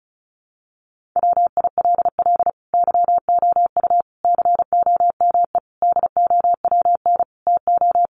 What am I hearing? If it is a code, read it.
Morse code, heard as a keyed tone, about 34 wpm, 700 Hz. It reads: WILL YOU COME DOWN TO
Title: Sentences from Top 1500 Words 35wpm